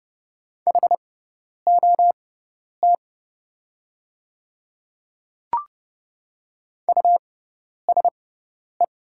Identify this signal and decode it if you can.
Morse code, heard as a keyed tone, about 31 wpm, spaced as for 10 wpm, 700 Hz. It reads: HOT USE
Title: 6x Character Spacing - Top 100 Words - 30wpm